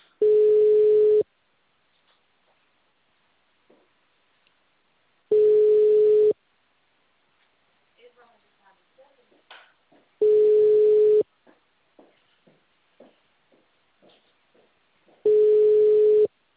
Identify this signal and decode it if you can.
Ringback tone (Europe, 425 Hz), with speech.
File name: out- -2020- - - .11361